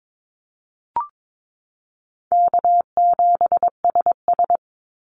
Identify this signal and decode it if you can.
Morse code, heard as a keyed tone, about 22 wpm, 700 Hz. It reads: K7SS
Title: Easy Call Signs Repeated 2x - 22wpm